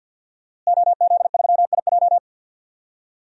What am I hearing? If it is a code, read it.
Morse code, heard as a keyed tone, about 53 wpm, 700 Hz. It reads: OZ3IJ
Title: DXCC-100 Call Signs - 5 Characters - Repeated 3x - 50wpm